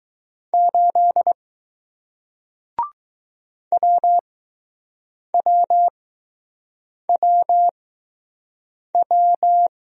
Morse code, heard as a keyed tone, about 20 wpm, spaced as for 6 wpm, 700 Hz. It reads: 8 WWWW